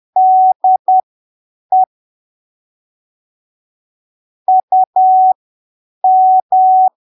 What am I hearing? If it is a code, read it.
Morse code, heard as a keyed tone, about 10 wpm, spaced as for 7 wpm, 750 Hz. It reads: DE UM
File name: arquivos / texto / texto 15 10ppm